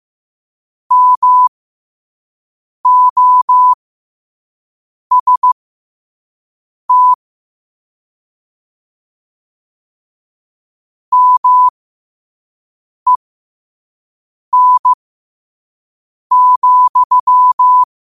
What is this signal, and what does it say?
Morse code, heard as a keyed tone, about 15 wpm, spaced as for 5 wpm, 1000 Hz. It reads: MOST MEN,
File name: qotdfiles / 5wpm